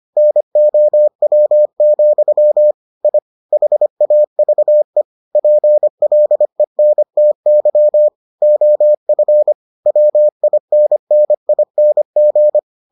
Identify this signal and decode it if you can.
Morse code, heard as a keyed tone, about 25 wpm, 600 Hz. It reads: NOW, I HAVE PLENTY OF WINNING